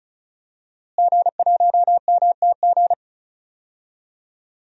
Morse code, heard as a keyed tone, about 36 wpm, 700 Hz. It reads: G1MTG
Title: DXCC-100 Call Signs - 5 Characters - Repeated 2x - 35wpm